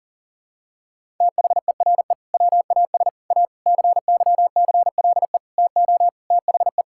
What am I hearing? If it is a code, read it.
Morse code, heard as a keyed tone, about 40 wpm, 700 Hz. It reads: THERE WAS A CYCLE TO THE
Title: Sentences from Top 1000 Words - Review Part 6 - 40wpm